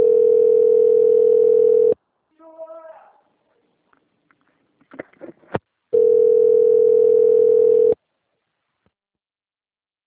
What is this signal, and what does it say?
Ringback tone (North America), 440 + 480 Hz.
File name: out- - - - - .10507